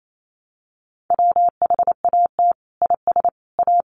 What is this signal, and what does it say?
Morse code, heard as a keyed tone, about 28 wpm, 700 Hz. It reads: WHAT IS A
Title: Sentences from Top 1400 Words 28wpm